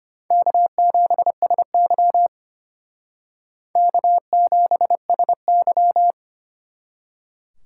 Morse code, heard as a keyed tone, about 32 wpm, 700 Hz. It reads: K7SY K7SY